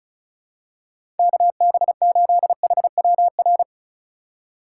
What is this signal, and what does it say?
Morse code, heard as a keyed tone, about 36 wpm, 700 Hz. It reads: KB8HWR
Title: USA Call Signs - 6 Characters - 2x Character Spacing - 35wpm